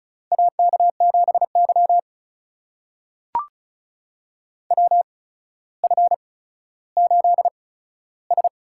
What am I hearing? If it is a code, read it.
Morse code, heard as a keyed tone, about 36 wpm, spaced as for 15 wpm, 700 Hz. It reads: AK7Y WF8S
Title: USA Call Signs - 4 Characters - 8x Character Spacing - 35wpm